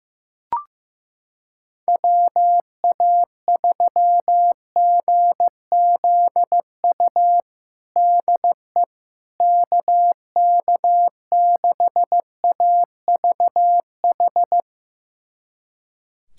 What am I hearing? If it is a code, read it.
Morse code, heard as a keyed tone, about 15 wpm, 700 Hz. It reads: WA3GZU DE KK6AVH
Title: USA Callsign DE Callsign - 6 Characters - 15wpm